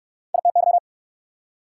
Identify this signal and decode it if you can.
Morse code, heard as a keyed tone, about 66 wpm, 690 Hz. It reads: ITY